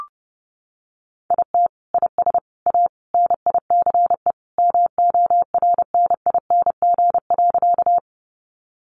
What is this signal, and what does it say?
Morse code, heard as a keyed tone, about 30 wpm, 700 Hz. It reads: IT IS A NICE MORNING.
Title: Sentences from Top 1100 Words 30wpm